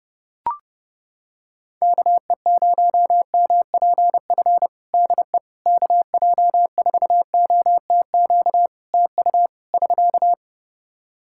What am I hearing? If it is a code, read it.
Morse code, heard as a keyed tone, about 28 wpm, 700 Hz. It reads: KE0MPF DE KJ4OTQ TU <SK>
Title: Practice QSO - Exchange 4 - Ending - 30wpm